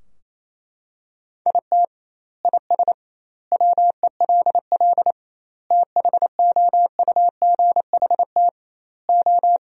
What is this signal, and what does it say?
Morse code, heard as a keyed tone, about 28 wpm, 700 Hz. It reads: IT IS WELL THOUGHT O
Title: 2x Word-Spacing - Sentences from Top 300 Words - 28wpm